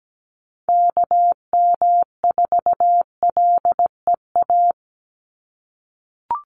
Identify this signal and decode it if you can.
Morse code, heard as a keyed tone, about 17 wpm, 700 Hz. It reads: KM4LEA
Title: USA Call Signs 17wpm